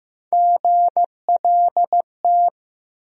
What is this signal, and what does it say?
Morse code, heard as a keyed tone, about 15 wpm, 700 Hz. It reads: GLT
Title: USA Call Sign Postfixes - 3 Letters 15wpm